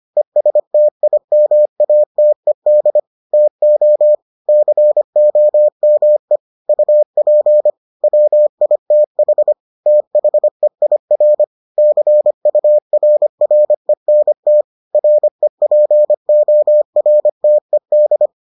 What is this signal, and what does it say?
Morse code, heard as a keyed tone, about 25 wpm, 600 Hz. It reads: ESTIMATED TO COME UP WITH THEIR CURRENT REPORTED